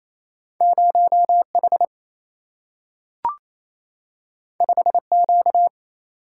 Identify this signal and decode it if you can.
Morse code, heard as a keyed tone, about 28 wpm, 700 Hz. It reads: 0 H 5 Q
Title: Single-Letter Single-Number 28wpm